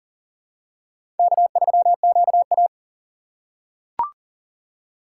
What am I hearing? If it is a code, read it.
Morse code, heard as a keyed tone, about 40 wpm, 700 Hz. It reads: K3QA
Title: USA Call Signs - 3x Character Spacing - 40wpm